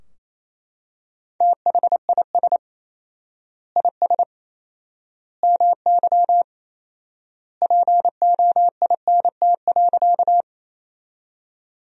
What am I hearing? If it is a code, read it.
Morse code, heard as a keyed tone, about 28 wpm, 700 Hz. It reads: THIS IS MY POINT.
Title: 4x Word-Spacing - Sentences from Top 200 Words - 28wpm